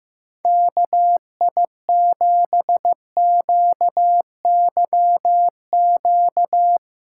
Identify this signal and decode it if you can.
Morse code, heard as a keyed tone, about 15 wpm, 700 Hz. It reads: KI7QYQ
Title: USA Call Signs - 6 Characters - 6x Character Spacing - 15wpm